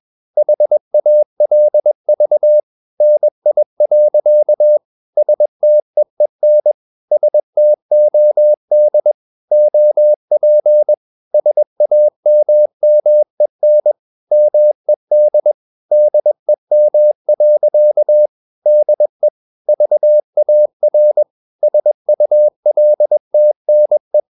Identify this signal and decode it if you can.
Morse code, heard as a keyed tone, about 21 wpm, 600 Hz. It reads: HALV NI. STEEN STOD OP SAMMEN MED DEM. DE VAR SULTNE
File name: Ferie QRS 21wpm